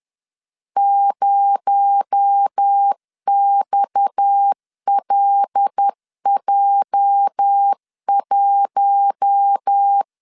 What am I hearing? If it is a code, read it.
Morse code, heard as a keyed tone, about 10 wpm, 800 Hz. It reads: 0XLJ1
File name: LEZIONI / 8 YJ45 / 8 YJ 10wpm 2651